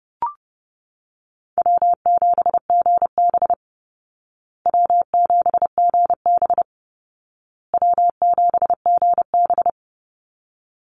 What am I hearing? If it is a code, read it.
Morse code, heard as a keyed tone, about 30 wpm, 700 Hz. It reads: W7GB W7GB W7GB
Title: Easy Call Signs Repeated 3x - 30wpm